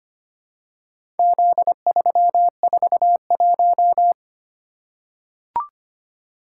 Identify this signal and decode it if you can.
Morse code, heard as a keyed tone, about 25 wpm, 700 Hz. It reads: Z341
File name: practice / 25wpm / Four Letter-Number Combinations - 25wpm